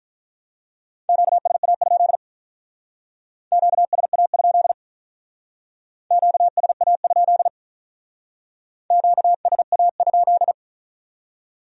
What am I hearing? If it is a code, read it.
Morse code, heard as a keyed tone, about 44 wpm, 700 Hz. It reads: QSA? QSA? QSA? QSA?
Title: Speed-Racing Q-Codes - 35wpm